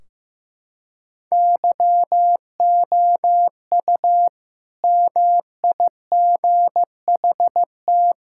Morse code, heard as a keyed tone, about 15 wpm, 700 Hz. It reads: YOU MIGHT
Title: Sentences from Top 1600 Words 15wpm